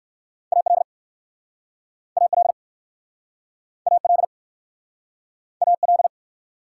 Morse code, heard as a keyed tone, about 60 wpm, 700 Hz. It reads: AL AL AL AL